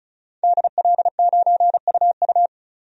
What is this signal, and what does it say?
Morse code, heard as a keyed tone, about 34 wpm, 700 Hz. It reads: DL9UU
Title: DXCC-100 Call Signs - 5 Characters - 35wpm